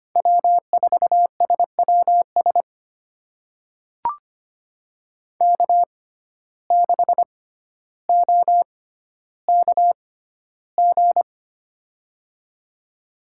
Morse code, heard as a keyed tone, about 25 wpm, spaced as for 13 wpm, 700 Hz. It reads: W4SWS K6OKG
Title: USA Call Signs - 5 Characters - 6x Character Spacing - 25wpm